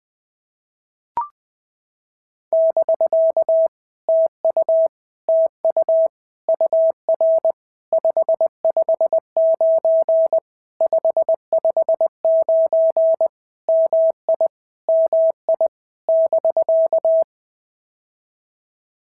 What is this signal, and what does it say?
Morse code, heard as a keyed tone, about 20 wpm, 650 Hz. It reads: <BK> TU TU UR 559 559 MI MI <BK>
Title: POTA QSO Practice - 20wpm